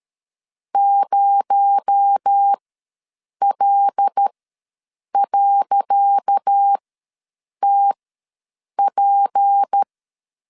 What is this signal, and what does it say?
Morse code, heard as a keyed tone, about 13 wpm, spaced as for 7 wpm, 800 Hz. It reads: 0L.TP
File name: LEZIONI / 10 SIGNS / 10C DOT ws 12wpm 3548